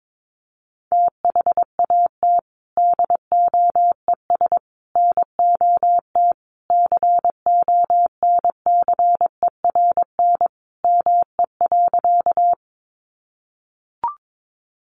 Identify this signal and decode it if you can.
Morse code, heard as a keyed tone, about 22 wpm, 700 Hz. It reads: THAT DOES NOT CONCERN ME.
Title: Sentences from Top 1000 Words 22wpm